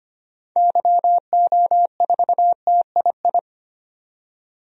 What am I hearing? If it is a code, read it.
Morse code, heard as a keyed tone, about 25 wpm, 700 Hz. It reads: YO4TII
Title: DXCC-100 Call Signs - 6 Characters - 25wpm